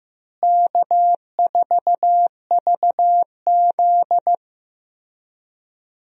Morse code, heard as a keyed tone, about 15 wpm, 700 Hz. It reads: K4VZ
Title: USA Call Signs - 4 Characters - Repeated 3x - 15wpm